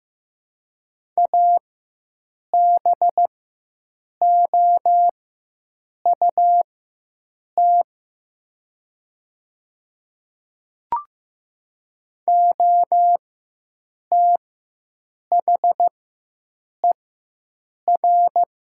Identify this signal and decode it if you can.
Morse code, heard as a keyed tone, about 15 wpm, spaced as for 7 wpm, 700 Hz. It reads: ABOUT OTHER
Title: 4x Character Spacing - Top 100 Words - 15wpm